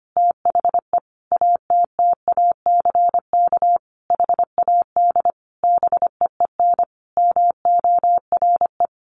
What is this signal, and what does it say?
Morse code, heard as a keyed tone, about 25 wpm, 700 Hz. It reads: THE ATTACK HAD BEEN MORE
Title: Sentences from Top 1500 Words 25wpm